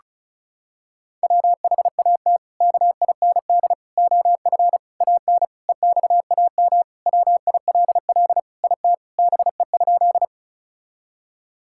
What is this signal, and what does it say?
Morse code, heard as a keyed tone, about 36 wpm, 700 Hz. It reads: WHAT KIND OF AN EXAM WILL IT BE?
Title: Sentences from Top 1000 Words - Review Part 5 - 35wpm